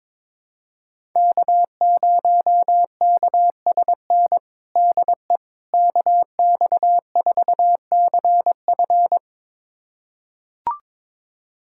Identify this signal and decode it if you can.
Morse code, heard as a keyed tone, about 22 wpm, 700 Hz. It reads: K0KSN DE KX4CF
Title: USA Callsign DE Callsign 22wpm